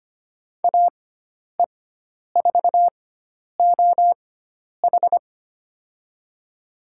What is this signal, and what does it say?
Morse code, heard as a keyed tone, about 25 wpm, spaced as for 10 wpm, 700 Hz. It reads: AE4OH
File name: Lesson 16g - Letter c - Call Signs 25wpm